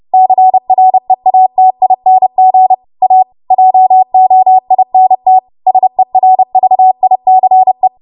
Morse code, heard as a keyed tone, about 30 wpm, 750 Hz. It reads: CREATING A JOINT SERVICE
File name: Archive / 30 WPM / 30WPM